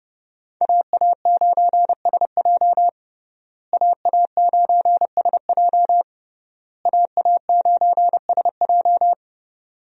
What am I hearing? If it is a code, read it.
Morse code, heard as a keyed tone, about 30 wpm, 700 Hz. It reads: AA9SJ AA9SJ AA9SJ
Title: USA Call Signs - 5 Characters - Repeated 3x - 30wpm